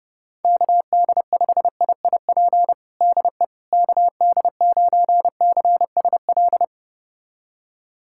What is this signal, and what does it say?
Morse code, heard as a keyed tone, about 30 wpm, 700 Hz. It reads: KD5IIP DE KD9CSL
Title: USA Callsign DE Callsign - 6 Characters - 30wpm